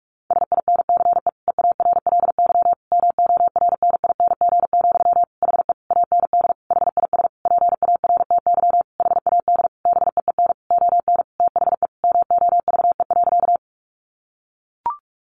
Morse code, heard as a keyed tone, about 44 wpm, 700 Hz. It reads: SINCE EARLY MORNING, HE AND HIS PARTY HAD BEEN ON THE MOVE.
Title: Sentences from Top 1400 Words 45wpm